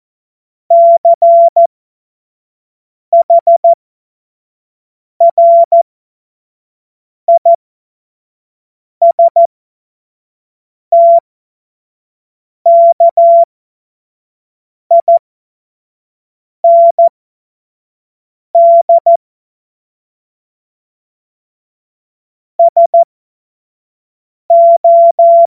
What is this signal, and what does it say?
Morse code, heard as a keyed tone, about 14 wpm, spaced as for 5 wpm, 680 Hz. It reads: CHRISTKIND SO
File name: Morsekurs / Witze / Witz 16. November 2021, 25 F70